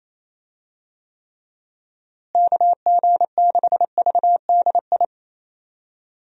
Morse code, heard as a keyed tone, about 28 wpm, 700 Hz. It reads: KG6VDI
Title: USA Call Signs - 6 Characters - Repeated 3x - 28wpm